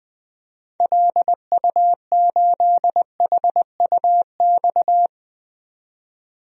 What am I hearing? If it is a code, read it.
Morse code, heard as a keyed tone, about 20 wpm, 700 Hz. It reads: LU8HUX
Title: DXCC-100 Call Signs - 6 Characters - 20wpm